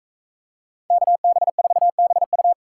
Morse code, heard as a keyed tone, about 42 wpm, 700 Hz. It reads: KB4BU